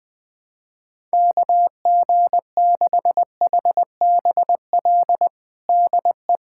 Morse code, heard as a keyed tone, about 21 wpm, 700 Hz. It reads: KG6HBL DE
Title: K1USN SST QSO Practice 20wpm